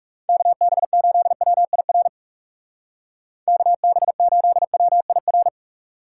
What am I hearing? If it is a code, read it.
Morse code, heard as a keyed tone, about 42 wpm, 700 Hz. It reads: KB8WIR KB8WIR